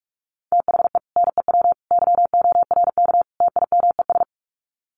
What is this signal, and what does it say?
Morse code, heard as a keyed tone, about 44 wpm, 700 Hz. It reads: THE NEW YORK TIMES
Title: Sets of 4 Words 45wpm